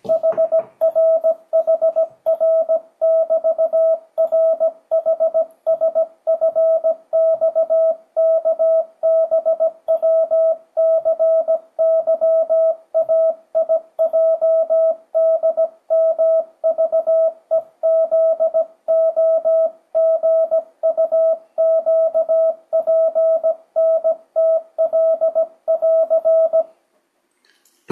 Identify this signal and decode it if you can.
Morse code, heard as a keyed tone, about 17 wpm, 650 Hz. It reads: HRHR<BT>RHSFXKBWCYAIJDMVEZOGUQPNTL<AR>
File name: 2025cw / speedup / 17wpm